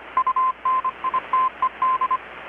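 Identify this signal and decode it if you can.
Morse code, heard as a keyed tone, about 25 wpm, 1040 Hz. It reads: UNITED
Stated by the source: Morse from KPH's Night of Nights XX maritime radiotelegraphy event, received in Chiapas, Mexico about 3,800 km away on a Kenwood R-600 with a grounded 10 m long-wire antenna.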